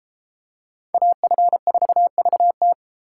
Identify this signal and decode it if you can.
Morse code, heard as a keyed tone, about 30 wpm, 700 Hz. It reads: AF4VT